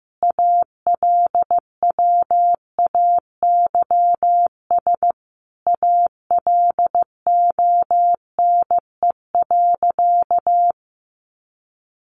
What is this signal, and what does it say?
Morse code, heard as a keyed tone, about 15 wpm, 700 Hz. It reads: ALWAYS ALONE.
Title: Sentences from Top 1000 Words 15wpm